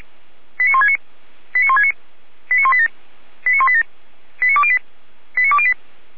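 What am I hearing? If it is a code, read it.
Five-tone selective call (ZVEI), heard as quick runs of five tones, tones of about 70 ms. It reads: 89169 89169 89187 89187 89208 89208
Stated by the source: ZVEI five-tone selective call received with an SDR.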